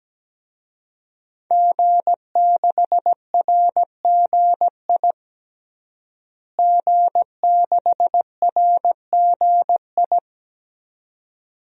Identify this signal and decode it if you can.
Morse code, heard as a keyed tone, about 17 wpm, 700 Hz. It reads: G6RGI G6RGI
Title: DXCC-100 Call Signs - 5 Characters - Repeated 2x - 17wpm